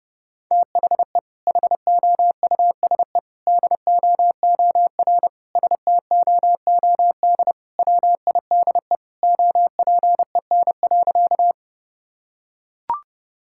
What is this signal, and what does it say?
Morse code, heard as a keyed tone, about 30 wpm, 700 Hz. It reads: THE HOUSE DOOR STOOD WIDE OPEN.
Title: Sentences from Top 1000 Words - Review Part 6 - 30wpm